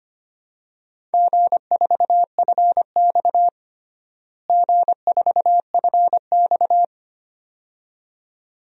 Morse code, heard as a keyed tone, about 25 wpm, 700 Hz. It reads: G4FX G4FX
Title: DXCC-100 Call Signs - 4 Characters - Repeated 2x - 25wpm